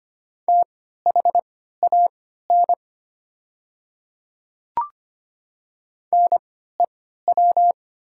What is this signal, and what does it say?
Morse code, heard as a keyed tone, about 25 wpm, spaced as for 14 wpm, 700 Hz. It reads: THAN NEW
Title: 3x Character Spacing - Top 300 Words - 25wpm